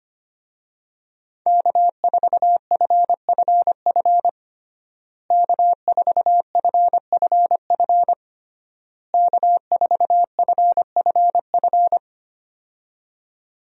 Morse code, heard as a keyed tone, about 25 wpm, 700 Hz. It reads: K4FFF K4FFF K4FFF
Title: USA Call Signs - 5 Characters - Repeated 3x - 25wpm